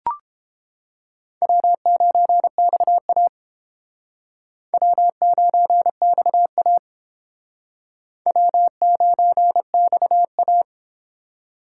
Morse code, heard as a keyed tone, about 32 wpm, 700 Hz. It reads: W9XA W9XA W9XA